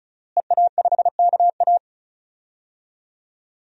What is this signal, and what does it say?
Morse code, heard as a keyed tone, about 36 wpm, 700 Hz. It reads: EA5KA
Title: DXCC-100 Call Signs - 5 Characters - Repeated 3x - 35wpm